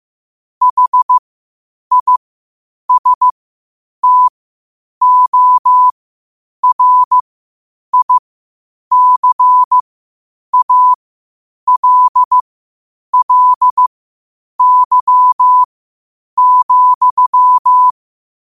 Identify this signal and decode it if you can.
Morse code, heard as a keyed tone, about 15 wpm, spaced as for 8 wpm, 1000 Hz. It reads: HISTORICALLY,
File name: qotdfiles / 8wpm